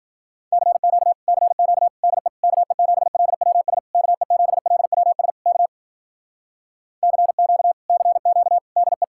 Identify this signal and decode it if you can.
Morse code, heard as a keyed tone, about 55 wpm, 700 Hz. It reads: CQ CQ DE KE7LWS KE7LWS K CQ CQ DE